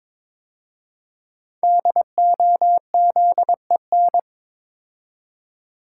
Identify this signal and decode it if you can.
Morse code, heard as a keyed tone, about 23 wpm, 700 Hz. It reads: DOZEN
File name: 5 Letter Words - First 3 Letters Given - 22wpm